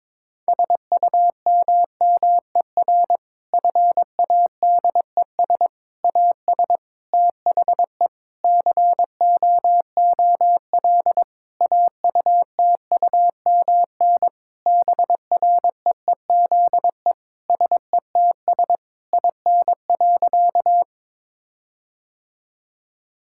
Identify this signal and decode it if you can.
Morse code, heard as a keyed tone, about 21 wpm, 700 Hz. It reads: SUMMER FADES AS THE COOL AUTUMN BREEZE SETS IN.
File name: Difficult Sentences with Hints - 22wpm